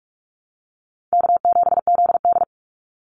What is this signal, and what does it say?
Morse code, heard as a keyed tone, about 44 wpm, 700 Hz. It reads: K7ZD
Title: Easy Call Signs - 45wpm